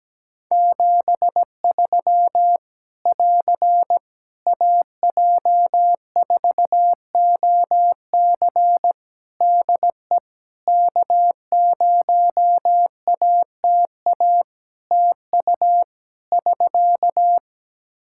Morse code, heard as a keyed tone, about 17 wpm, 700 Hz. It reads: 73 <AR> AJ4OC DE K0ATA TU <SK>
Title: Practice QSO - Exchange 4 - Ending - 17wpm